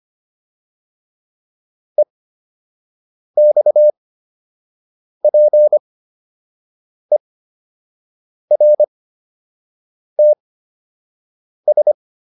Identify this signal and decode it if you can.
Morse code, heard as a keyed tone, about 25 wpm, spaced as for 6 wpm, 600 Hz. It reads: EXPERTS